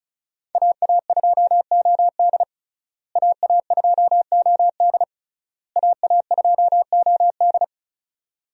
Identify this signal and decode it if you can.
Morse code, heard as a keyed tone, about 34 wpm, 700 Hz. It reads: AA2OD AA2OD AA2OD
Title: USA Call Signs - 5 Characters - Repeated 3x - 35wpm